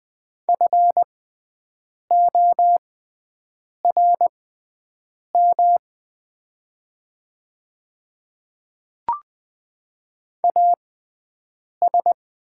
Morse code, heard as a keyed tone, about 20 wpm, spaced as for 7 wpm, 700 Hz. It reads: FORM AS
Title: 6x Character Spacing - Top 200 Words - 20wpm